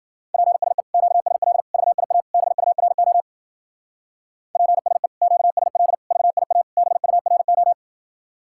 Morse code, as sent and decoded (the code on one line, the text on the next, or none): .--. ... / --.- ... .-.. / ...- .. .- / -... ..- .-. --- / .--. ... / --.- ... .-.. / ...- .. .- / -... ..- .-. ---
PSE QSL VIA BURO PSE QSL VIA BURO